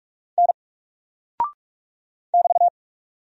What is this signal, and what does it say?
Morse code, heard as a keyed tone, about 44 wpm, 700 Hz. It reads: N <BT>
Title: Single Character (Letter:Number:Punctuation:Prosign) 45wpm